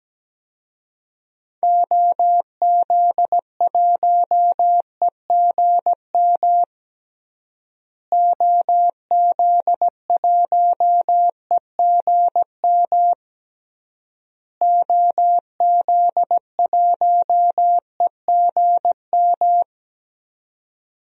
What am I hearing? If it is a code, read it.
Morse code, heard as a keyed tone, about 17 wpm, 700 Hz. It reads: OZ1EGM OZ1EGM OZ1EGM
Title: DXCC-100 Call Signs - 6 Characters - Repeated 3x - 17wpm